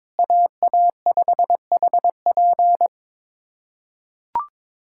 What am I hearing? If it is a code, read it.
Morse code, heard as a keyed tone, about 22 wpm, 700 Hz. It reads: AA5HP
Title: USA Callsign DE Callsign - 5 Characters - 22wpm